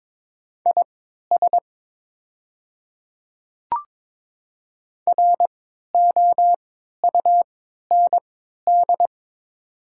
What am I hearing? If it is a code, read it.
Morse code, heard as a keyed tone, about 22 wpm, spaced as for 12 wpm, 700 Hz. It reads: IS ROUND
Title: 3x Character Spacing - Top 300 Words - 22wpm